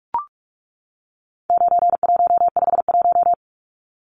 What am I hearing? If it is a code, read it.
Morse code, heard as a keyed tone, about 44 wpm, 700 Hz. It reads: 9151
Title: Numbers - 4-Digit 45wpm